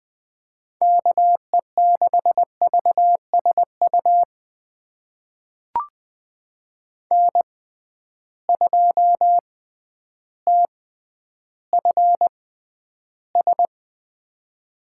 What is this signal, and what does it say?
Morse code, heard as a keyed tone, about 20 wpm, spaced as for 11 wpm, 700 Hz. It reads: KE6VSU N2TFS
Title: USA Call Signs - 6x Character Spacing - 20wpm